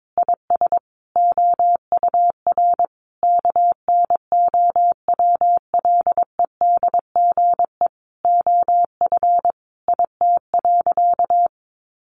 Morse code, heard as a keyed tone, about 22 wpm, 700 Hz. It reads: IS OUR KNOWLEDGE OF IT.
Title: Sentences from Top 1400 Words 22wpm